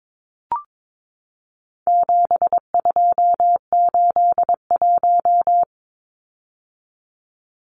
Morse code, heard as a keyed tone, about 22 wpm, 700 Hz. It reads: 7281
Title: Numbers - 4-Digit 22wpm